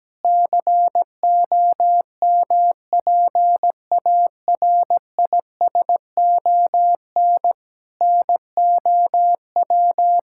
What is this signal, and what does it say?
Morse code, heard as a keyed tone, about 17 wpm, 700 Hz. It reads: COMPARISON NOW
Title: Sentences from Top 1600 Words 17wpm